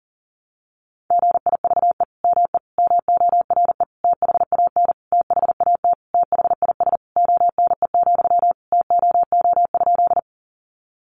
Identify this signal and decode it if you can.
Morse code, heard as a keyed tone, about 40 wpm, 700 Hz. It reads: GIVE ME MORE THAN THAT THIS ONE, TOO?
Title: Sentences from Top 200 Words 40wpm